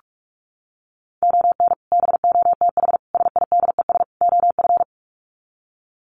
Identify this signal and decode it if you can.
Morse code, heard as a keyed tone, about 44 wpm, 700 Hz. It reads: ON BOTH SIDES OF
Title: Sets of 4 Words 45wpm